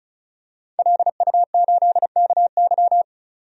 Morse code, heard as a keyed tone, about 36 wpm, 700 Hz. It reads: LU8KY
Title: DXCC-100 Call Signs - 5 Characters - 35wpm